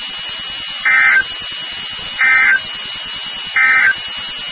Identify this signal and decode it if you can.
Emergency Alert System (SAME) tones, heard as harsh buzzing bursts. It is heard from 0.7 to 4.0 s.
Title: NOAA Weather Radio WXK36